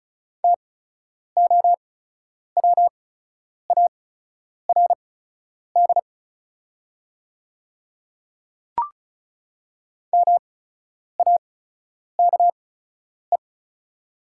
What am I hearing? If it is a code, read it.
Morse code, heard as a keyed tone, about 36 wpm, spaced as for 10 wpm, 690 Hz. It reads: TOWARD MAKE
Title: 8x Character Spacing - Top 500 Words - 35wpm